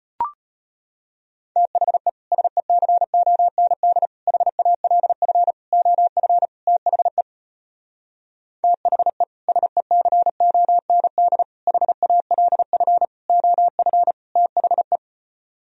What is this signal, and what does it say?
Morse code, heard as a keyed tone, about 33 wpm, 700 Hz. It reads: THE SECOND HALF OF THE THE SECOND HALF OF THE